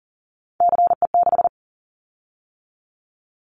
Morse code, heard as a keyed tone, about 40 wpm, 700 Hz. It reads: CE6
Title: Callsign Prefixes for Top 100 DXCC - 40wpm